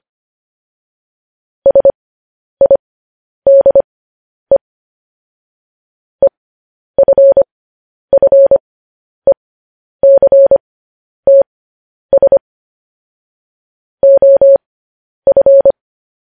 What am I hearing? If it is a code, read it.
Morse code, heard as a keyed tone, about 25 wpm, spaced as for 10 wpm, 550 Hz. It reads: SIDE EFFECTS OF
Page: news-headlines-in-morse-code-at-10-wpm